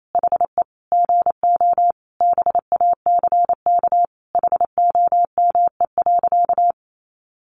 Morse code, heard as a keyed tone, about 28 wpm, 700 Hz. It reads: HE GO BACK HOME.
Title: Sentences from Top 500 Words 28wpm